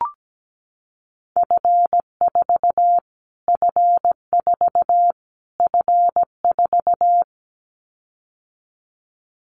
Morse code, heard as a keyed tone, about 17 wpm, 700 Hz. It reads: F4 F4 F4
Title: Callsign Prefixes for Top 25 DXCC - Repeated 3x - 17wpm